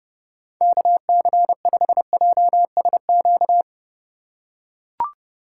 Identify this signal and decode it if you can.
Morse code, heard as a keyed tone, about 29 wpm, 700 Hz. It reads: KC5JSQ
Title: USA Call Signs - 6x Character Spacing - 30wpm